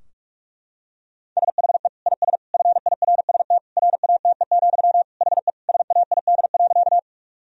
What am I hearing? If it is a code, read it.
Morse code, heard as a keyed tone, about 41 wpm, 700 Hz. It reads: SHE IS FIRST RATE, HE SAID.
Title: Sentences from Top 1000 Words - Review Part 4 - 45wpm